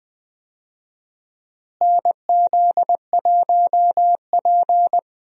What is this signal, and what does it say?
Morse code, heard as a keyed tone, about 20 wpm, 700 Hz. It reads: NZ1P
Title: USA Call Signs - 4 Characters - Repeated 2x - 20wpm